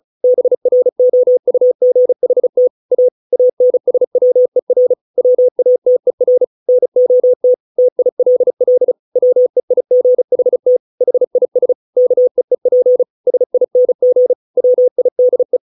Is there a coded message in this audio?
Yes: Morse code from 0.2 to 15.6 s, heard as a keyed tone, about 34 wpm, 500 Hz. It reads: BROUGHT A ANSWER WATER NOT TILL WEIGHT HIS KEEP SING WIDE